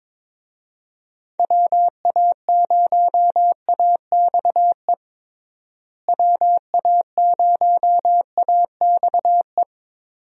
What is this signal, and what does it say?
Morse code, heard as a keyed tone, about 22 wpm, 700 Hz. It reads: WA0AXE WA0AXE
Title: USA Call Signs - 6 Characters - Repeated 2x - 22wpm